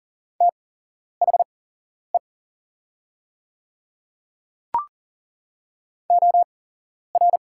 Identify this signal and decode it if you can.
Morse code, heard as a keyed tone, about 40 wpm, spaced as for 11 wpm, 690 Hz. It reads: THE OR
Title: 8x Character Spacing - Top 100 Words - 40wpm